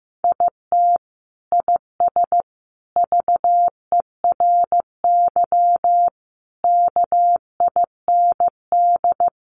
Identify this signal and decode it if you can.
Morse code, heard as a keyed tone, about 15 wpm, 700 Hz. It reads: IT IS VERY KIND
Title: Sentences from Top 200 Words 15wpm